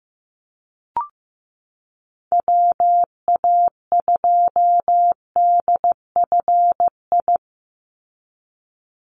Morse code, heard as a keyed tone, about 15 wpm, 700 Hz. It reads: WA2DFI
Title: Call Signs Repeated 2x - 15wpm